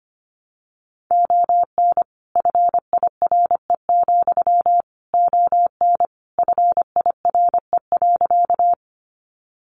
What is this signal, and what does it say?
Morse code, heard as a keyed tone, about 25 wpm, 700 Hz. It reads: ON FIRE, ON FIRE.
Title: Sentences from Top 400 Words 25wpm